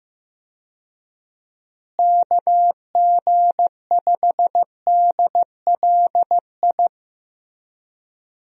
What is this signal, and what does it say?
Morse code, heard as a keyed tone, about 15 wpm, 700 Hz. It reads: KG5DLI
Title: USA Call Signs - 6 Characters - Repeated 3x - 15wpm